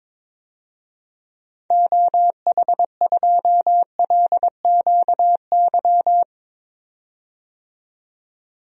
Morse code, heard as a keyed tone, about 22 wpm, 700 Hz. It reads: OH2LQY
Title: DXCC-100 Call Signs - 6 Characters - 22wpm